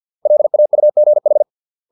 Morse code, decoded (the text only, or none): LAUGH